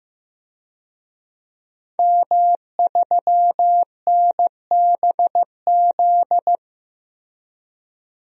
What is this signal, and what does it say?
Morse code, heard as a keyed tone, about 15 wpm, 700 Hz. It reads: M3NBZ
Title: DXCC-100 Call Signs - 5 Characters - Repeated 2x - 15wpm